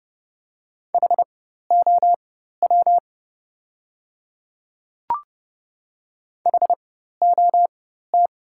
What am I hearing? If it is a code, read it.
Morse code, heard as a keyed tone, about 30 wpm, spaced as for 14 wpm, 700 Hz. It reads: HOW HOT
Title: 4x Character Spacing - Top 200 Words - 30wpm